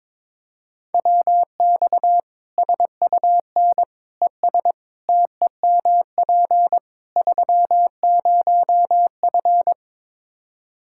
Morse code, heard as a keyed tone, about 22 wpm, 700 Hz. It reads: WX SUN ES TEMP 30F
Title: Practice QSO - Exchange 2 - Info - 22wpm